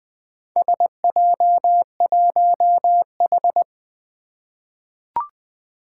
Morse code, heard as a keyed tone, about 20 wpm, 700 Hz. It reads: SJ1H
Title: Four Challenging Letter-Number Combinations 20wpm